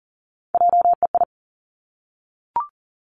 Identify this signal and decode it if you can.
Morse code, heard as a keyed tone, about 41 wpm, 700 Hz. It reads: JEI
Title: Three Challenging Letter-Number Combinations 40wpm